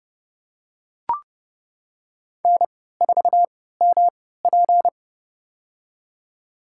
Morse code, heard as a keyed tone, about 30 wpm, spaced as for 17 wpm, 700 Hz. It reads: N4MP
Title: USA Call Signs - 4 Characters - 3x Character Spacing - 30wpm